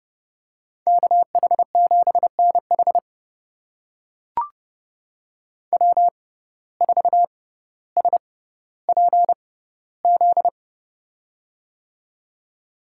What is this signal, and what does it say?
Morse code, heard as a keyed tone, about 30 wpm, spaced as for 15 wpm, 700 Hz. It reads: KH7NH W4SPZ
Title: USA Call Signs - 6x Character Spacing - 30wpm